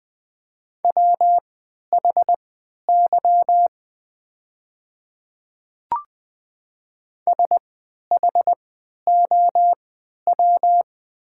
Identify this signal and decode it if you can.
Morse code, heard as a keyed tone, about 20 wpm, spaced as for 11 wpm, 700 Hz. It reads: WHY SHOW
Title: 3x Character Spacing - Top 200 Words - 20wpm